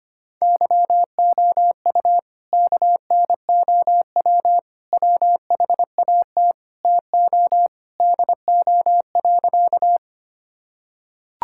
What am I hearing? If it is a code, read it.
Morse code, heard as a keyed tone, about 25 wpm, 700 Hz. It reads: YOU KNOW WHAT TO DO.
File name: practice / 25wpm / Sentences with 3-Word Phrases - 25wpm